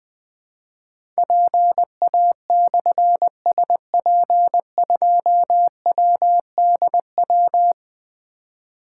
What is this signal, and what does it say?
Morse code, heard as a keyed tone, about 20 wpm, 700 Hz. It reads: PA/SP2WDW
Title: Difficult Call Signs 20wpm